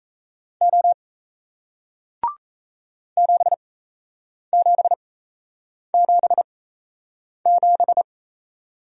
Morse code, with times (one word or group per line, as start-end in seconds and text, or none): 0.6-0.9 O
3.2-8.0 7777